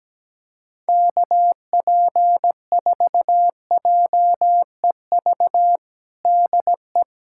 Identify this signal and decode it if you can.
Morse code, heard as a keyed tone, about 17 wpm, 700 Hz. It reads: KP4JEV DE
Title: USA Callsign DE Callsign 17wpm